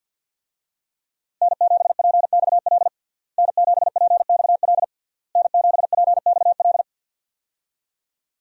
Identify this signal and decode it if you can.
Morse code, heard as a keyed tone, about 51 wpm, 700 Hz. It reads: N7PXL N7PXL N7PXL
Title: USA Call Signs - 5 Characters - Repeated 3x - 50wpm